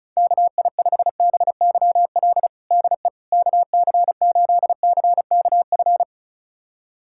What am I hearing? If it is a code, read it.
Morse code, heard as a keyed tone, about 36 wpm, 700 Hz. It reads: KI5BYL DE KC8CKF
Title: USA Callsign DE Callsign - 6 Characters - 35wpm